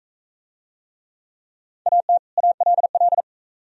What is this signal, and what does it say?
Morse code, heard as a keyed tone, about 35 wpm, 700 Hz. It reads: AT ALL
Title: Speed-Racing - Sets of 2 Words 28wpm